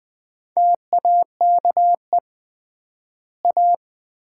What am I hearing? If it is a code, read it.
Morse code, heard as a keyed tone, about 21 wpm, 700 Hz. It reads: TAKE A
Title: 3x Word-Spacing - Sets of 4 Words - 20wpm